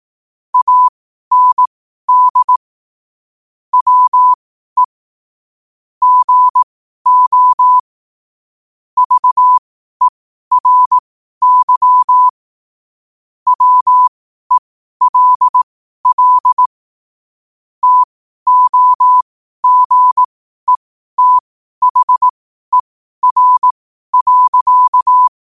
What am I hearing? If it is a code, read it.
Morse code, heard as a keyed tone, about 18 wpm, spaced as for 12 wpm, 1000 Hz. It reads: AND WE GO VERY WELL TOGETHER.
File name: qotdfiles / 12wpm